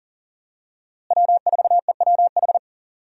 Morse code, heard as a keyed tone, about 40 wpm, 700 Hz. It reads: W4EWH
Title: USA Call Signs - 5 Characters - Repeated 3x - 40wpm